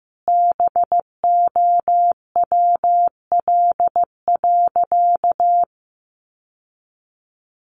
Morse code, heard as a keyed tone, about 15 wpm, 700 Hz. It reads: BOWL.
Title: Sentences from Top 1200 Words 15wpm